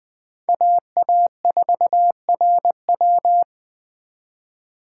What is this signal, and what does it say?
Morse code, heard as a keyed tone, about 24 wpm, 700 Hz. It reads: AA4RW
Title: Speed Racing - USA Call Signs 20wpm